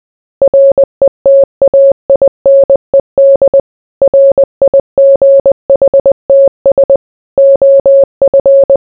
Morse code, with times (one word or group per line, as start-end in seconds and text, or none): 0.4-3.6 RETAINED
4.0-7.0 RIGHTS
7.4-8.8 OF